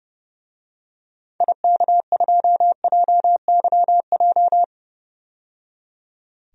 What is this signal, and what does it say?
Morse code, heard as a keyed tone, about 30 wpm, 700 Hz. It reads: IK2JYJ
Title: DXCC-100 Call Signs - 6 Characters - 30wpm